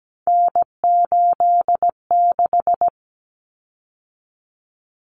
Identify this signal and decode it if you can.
Morse code, heard as a keyed tone, about 17 wpm, 700 Hz. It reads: N86
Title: Common Cut Numbers - 3-Digit 17wpm